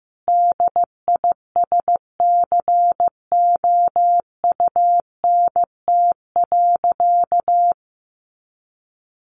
Morse code, heard as a keyed tone, about 15 wpm, 700 Hz. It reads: DISCOUNT.
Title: Sentences from Top 1200 Words 15wpm